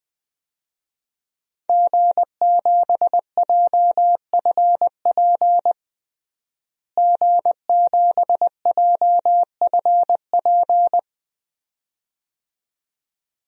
Morse code, heard as a keyed tone, about 20 wpm, 700 Hz. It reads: G7JFP G7JFP
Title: DXCC-100 Call Signs - 5 Characters - Repeated 2x - 20wpm